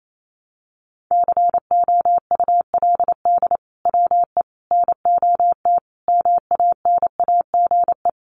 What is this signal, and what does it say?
Morse code, heard as a keyed tone, about 28 wpm, 700 Hz. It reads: COULD WE NOT MANAGE
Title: Sentences from Top 1400 Words 28wpm